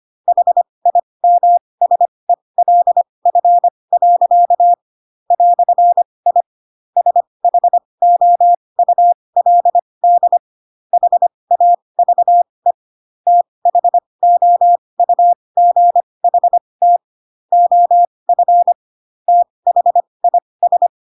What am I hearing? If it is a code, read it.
Morse code, heard as a keyed tone, about 25 wpm, spaced as for 20 wpm, 700 Hz. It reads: HIMSELF. "I SHOULD HAVE THOUGHT OF THIS